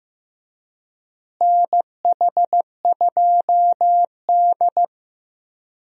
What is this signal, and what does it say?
Morse code, heard as a keyed tone, about 15 wpm, 700 Hz. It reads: NH2D
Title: USA Call Signs - 4 Characters - Repeated 3x - 15wpm